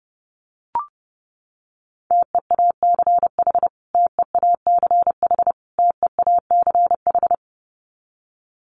Morse code, heard as a keyed tone, about 30 wpm, 700 Hz. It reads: TEACH TEACH TEACH
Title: Top 500 Words Repeated 30wpm